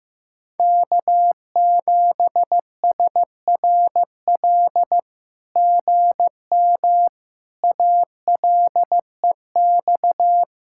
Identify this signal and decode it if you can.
Morse code, heard as a keyed tone, about 15 wpm, 700 Hz. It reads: K7SRL GM ALEX
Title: SOTA QSO Practice 15wpm